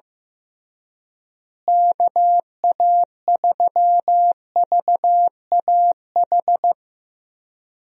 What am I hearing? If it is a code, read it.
Morse code, heard as a keyed tone, about 15 wpm, 700 Hz. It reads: KA3VAH
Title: USA Call Signs - 6 Characters - Repeated 2x - 15wpm